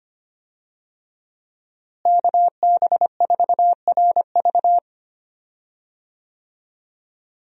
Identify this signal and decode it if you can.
Morse code, heard as a keyed tone, about 25 wpm, 700 Hz. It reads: KB4RV
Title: USA Call Signs - 5 Characters - Repeated 3x - 25wpm